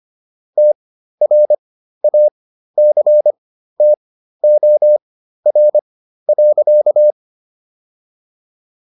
Morse code, heard as a keyed tone, about 25 wpm, spaced as for 13 wpm, 600 Hz. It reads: TRACTOR.